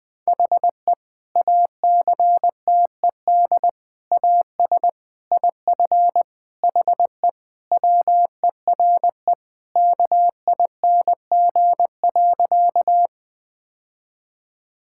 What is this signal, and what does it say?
Morse code, heard as a keyed tone, about 20 wpm, 700 Hz. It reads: HE ACTED AS IF HE WERE KING.
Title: Sentences with 4-Word Phrases - Spoken Beginning - 20wpm